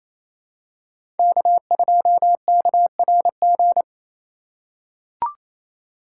Morse code, heard as a keyed tone, about 28 wpm, 700 Hz. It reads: K2KRG
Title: USA Call Signs - 8x Character Spacing - 28wpm